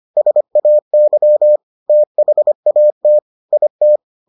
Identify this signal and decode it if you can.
Morse code, heard as a keyed tone, about 25 wpm, 600 Hz. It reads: SAY THAT IT